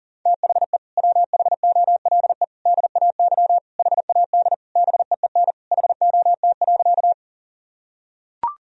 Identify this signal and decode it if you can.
Morse code, heard as a keyed tone, about 40 wpm, 700 Hz. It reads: THE WHOLE DAY HAD BEEN HOT.
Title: Sentences from Top 1000 Words - Review Part 3 - 40wpm